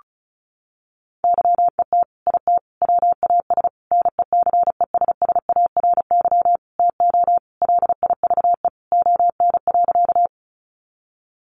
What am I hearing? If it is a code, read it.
Morse code, heard as a keyed tone, about 34 wpm, 700 Hz. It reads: YET IT WAS NECESSARY TO LIVE ON.
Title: Sentences from Top 800 Words 35wpm